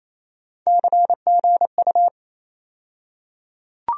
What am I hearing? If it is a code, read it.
Morse code, heard as a keyed tone, about 28 wpm, 700 Hz. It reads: CGU
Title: USA Call Sign Postfixes - 3 Letters 28wpm